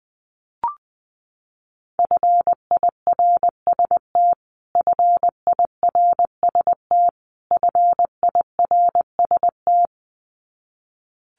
Morse code, heard as a keyed tone, about 20 wpm, 700 Hz. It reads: FIRST FIRST FIRST
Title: Top 500 Words Repeated 20wpm